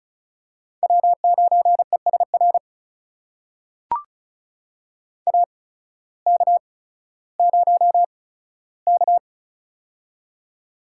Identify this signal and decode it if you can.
Morse code, heard as a keyed tone, about 36 wpm, spaced as for 16 wpm, 700 Hz. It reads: W9ESR AK0K
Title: USA Call Signs - 8x Character Spacing - 35wpm